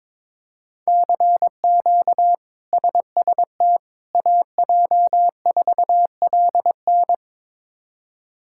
Morse code, heard as a keyed tone, about 22 wpm, 700 Hz. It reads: CQ SST AJ4LN
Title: K1USN SST QSO Practice - 22wpm